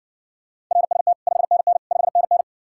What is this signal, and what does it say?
Morse code, heard as a keyed tone, about 60 wpm, 700 Hz. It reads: RST 5NN 5NN